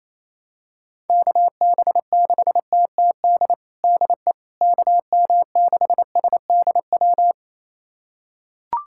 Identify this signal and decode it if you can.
Morse code, heard as a keyed tone, about 28 wpm, 700 Hz. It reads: KB6TTD DE KM6SDW
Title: USA Callsign DE Callsign - 6 Characters - 28wpm